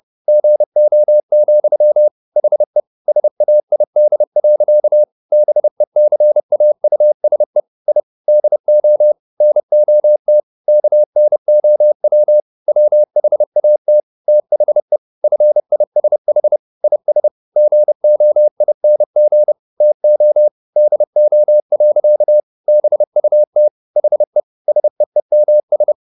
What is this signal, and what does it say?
Morse code, heard as a keyed tone, about 30 wpm, 600 Hz. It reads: GO, HE SAID. BECAUSE I DO NOT KNOW WHAT THE FISH IS GOING TO DO. BUT HE SEEMS
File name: the old man and the sea / the old man and the sea 1 30wpm